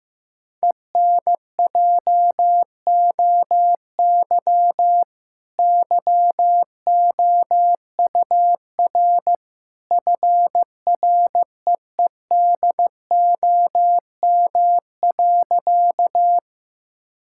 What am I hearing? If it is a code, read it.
Morse code, heard as a keyed tone, about 16 wpm, 700 Hz. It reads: ENJOY YOUR FREEDOM.
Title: Sentences from Top 1000 Words - Review Part 5 - 15wpm